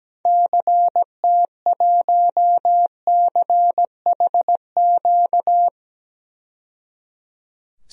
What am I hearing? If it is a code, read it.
Morse code, heard as a keyed tone, about 17 wpm, 700 Hz. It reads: CT1CHQ
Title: DXCC-100 Call Signs - 6 Characters - 17wpm